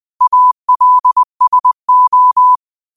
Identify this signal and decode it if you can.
Morse code, heard as a keyed tone, about 20 wpm, 1000 Hz. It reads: ALSO
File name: qotdfiles / 20wpm